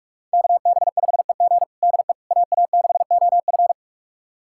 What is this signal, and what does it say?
Morse code, heard as a keyed tone, about 46 wpm, 700 Hz. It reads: KB5EG DE AA6OF
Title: USA Callsign DE Callsign - 5 Characters - 45wpm